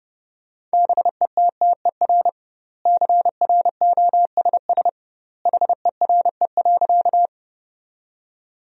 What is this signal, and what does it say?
Morse code, heard as a keyed tone, about 32 wpm, 700 Hz. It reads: BETTER CROSS HERE.
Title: 2x Word-Spacing - Sentences from Top 400 Words - 30wpm